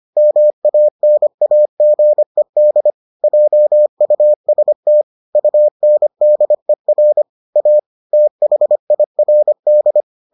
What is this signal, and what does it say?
Morse code, heard as a keyed tone, about 25 wpm, 600 Hz. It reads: MANAGED JUST UNDER A THIRD